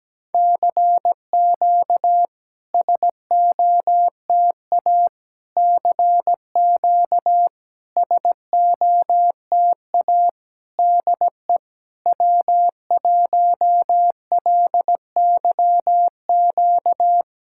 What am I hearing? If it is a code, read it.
Morse code, heard as a keyed tone, about 17 wpm, 690 Hz. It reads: CQ SOTA CQ SOTA DE W1LYQ
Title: SOTA QSO Practice 17wpm